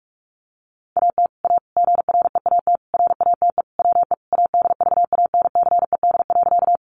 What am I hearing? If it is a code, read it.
Morse code, heard as a keyed tone, about 44 wpm, 700 Hz. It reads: AT A GREAT RATE WE ADVANCED.
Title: Sentences from Top 1200 Words 45wpm